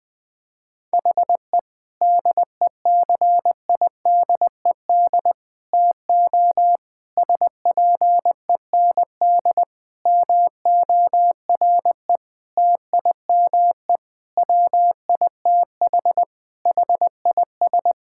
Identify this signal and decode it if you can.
Morse code, heard as a keyed tone, about 21 wpm, 700 Hz. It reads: HE DECIDED TO SPEND MORE TIME WITH HIS
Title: Sentences with 7-Word Phrases - 20wpm